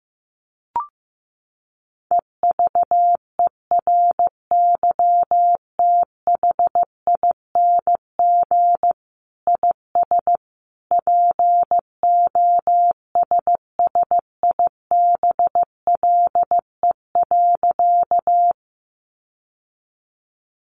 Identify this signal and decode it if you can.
Morse code, heard as a keyed tone, about 15 wpm, 700 Hz. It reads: EVERYTHING IS POSSIBLE.
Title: Sentences from Top 700 Words 15wpm